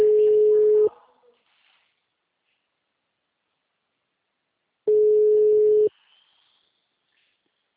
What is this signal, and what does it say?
Ringback tone (Europe, 425 Hz), with speech.